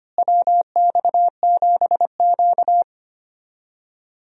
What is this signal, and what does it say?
Morse code, heard as a keyed tone, about 25 wpm, 700 Hz. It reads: WX7Q
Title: USA Call Signs - 4 Characters - Repeated 3x - 25wpm